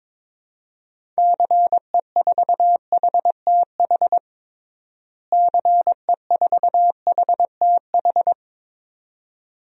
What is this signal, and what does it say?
Morse code, heard as a keyed tone, about 22 wpm, 700 Hz. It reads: CE4HTH CE4HTH
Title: DXCC-100 Call Signs - 6 Characters - Repeated 2x - 22wpm